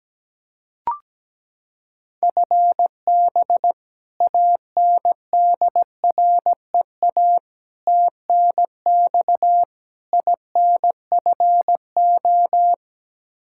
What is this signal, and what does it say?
Morse code, heard as a keyed tone, about 17 wpm, 700 Hz. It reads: FB ANDREA TNX INFO
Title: Practice QSO - Exchange 3 - More Info - 17wpm